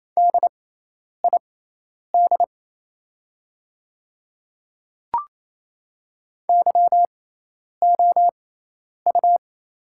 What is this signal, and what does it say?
Morse code, heard as a keyed tone, about 28 wpm, spaced as for 10 wpm, 700 Hz. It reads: DID YOU
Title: 6x Character Spacing - Top 100 Words - 28wpm